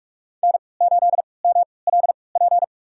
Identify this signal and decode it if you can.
Morse code, heard as a keyed tone, about 46 wpm, spaced as for 25 wpm, 700 Hz. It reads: N8MLP
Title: USA Call Signs - 3x Character Spacing - 45wpm